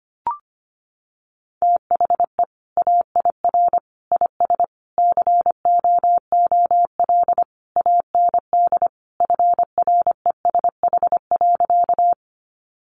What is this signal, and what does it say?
Morse code, heard as a keyed tone, about 25 wpm, 700 Hz. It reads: THE AIR IS COOL AND FRESH.
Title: Sentences from Top 1200 Words 25wpm